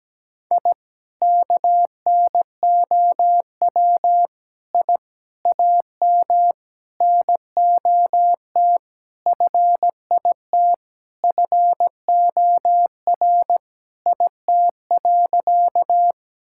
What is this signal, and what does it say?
Morse code, heard as a keyed tone, about 17 wpm, 700 Hz. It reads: I KNOW I AM NOT FIT FOR IT.
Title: Sentences from Top 1000 Words - Review Part 5 - 17wpm